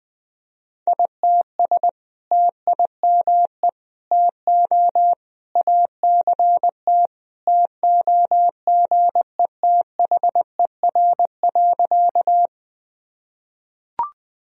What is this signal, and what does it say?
Morse code, heard as a keyed tone, about 20 wpm, 700 Hz. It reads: ITS TIME TO ACT TOGETHER.
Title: Sentences from Top 1000 Words - Review Part 1 - 20wpm